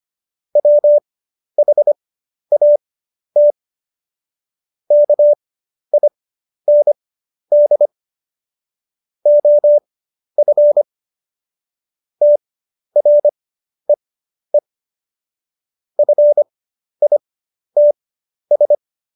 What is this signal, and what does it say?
Morse code, heard as a keyed tone, about 25 wpm, spaced as for 11 wpm, 590 Hz. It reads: WHAT KIND OF TREE FITS
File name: Jokes-25-char-10-eff